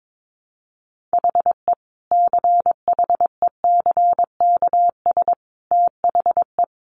Morse code, heard as a keyed tone, about 22 wpm, 700 Hz. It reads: HE CHECKS THE
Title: Easy Phrases 22wpm